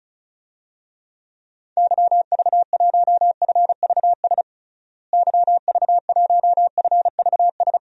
Morse code, heard as a keyed tone, about 36 wpm, 700 Hz. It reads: YV1FVS YV1FVS
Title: DXCC-100 Call Signs - 6 Characters - Repeated 2x - 35wpm